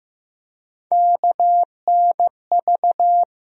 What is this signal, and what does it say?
Morse code, heard as a keyed tone, about 15 wpm, 700 Hz. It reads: KNV
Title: USA Call Sign Postfixes - 3 Letters 15wpm